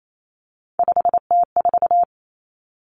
Morse code, heard as a keyed tone, about 28 wpm, 700 Hz. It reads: 5T4
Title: Three Challenging Letter-Number Combinations 28wpm